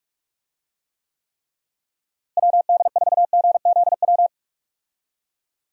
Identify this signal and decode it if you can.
Morse code, heard as a keyed tone, about 42 wpm, 700 Hz. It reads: WD4GZW
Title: USA Call Signs - 6 Characters - Repeated 3x - 45wpm